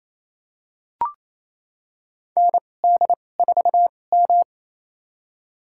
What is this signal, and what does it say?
Morse code, heard as a keyed tone, about 28 wpm, spaced as for 20 wpm, 700 Hz. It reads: ND4M
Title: USA Call Signs - 4 Characters - 2x Character Spacing - 28wpm